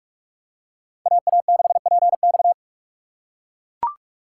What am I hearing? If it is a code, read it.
Morse code, heard as a keyed tone, about 42 wpm, 700 Hz. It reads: AA6PX